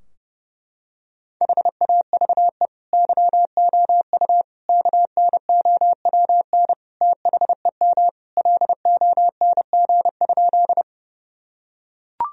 Morse code, heard as a keyed tone, about 30 wpm, 700 Hz. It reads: HAVE YOU KNOWN THEM LONG?
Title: Sentences from Top 1000 Words - Review Part 3 - 30wpm